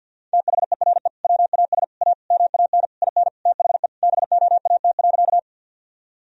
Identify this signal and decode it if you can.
Morse code, heard as a keyed tone, about 52 wpm, 700 Hz. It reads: THERE WAS A MAN IN THE BOAT.
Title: Sentences from Top 1000 Words - Review Part 3 - 50wpm